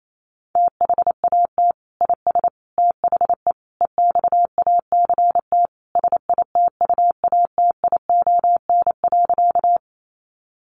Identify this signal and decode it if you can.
Morse code, heard as a keyed tone, about 28 wpm, 700 Hz. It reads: THAT IS THE EXACT SITUATION.
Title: Sentences from Top 1300 Words 28wpm